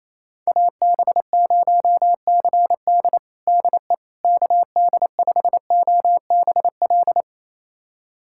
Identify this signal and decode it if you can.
Morse code, heard as a keyed tone, about 27 wpm, 700 Hz. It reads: AB0CD DE KD5OBL
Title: USA Callsign DE Callsign 28wpm